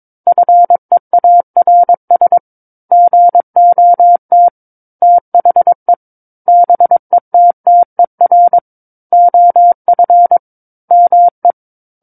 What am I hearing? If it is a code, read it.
Morse code, heard as a keyed tone, about 22 wpm, 700 Hz. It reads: FEARS GOT THE BETTER OF ME
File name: Book One / 13-Chapter 13 - 22 WPM